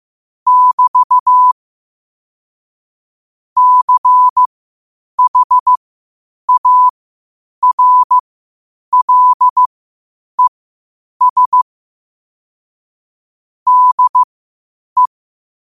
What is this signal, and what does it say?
Morse code, heard as a keyed tone, about 15 wpm, spaced as for 8 wpm, 1000 Hz. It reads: <BT> CHARLES DE
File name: qotdfiles / 8wpm